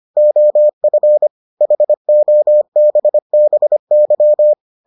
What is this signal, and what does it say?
Morse code, heard as a keyed tone, about 25 wpm, 600 Hz. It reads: OF HOBBY